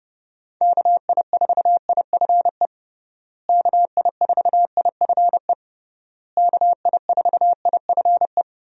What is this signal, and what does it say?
Morse code, heard as a keyed tone, about 29 wpm, 700 Hz. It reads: KI4IFE KI4IFE KI4IFE
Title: USA Call Signs - 6 Characters - Repeated 3x - 30wpm